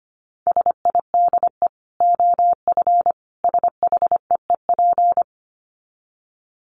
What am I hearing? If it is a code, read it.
Morse code, heard as a keyed tone, about 25 wpm, 700 Hz. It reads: SIDE OF SHEEP
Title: Dit Challenge 25wpm